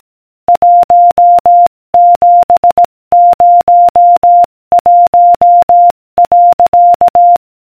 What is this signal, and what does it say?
Morse code, heard as a keyed tone, about 18 wpm, 700 Hz. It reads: 1701.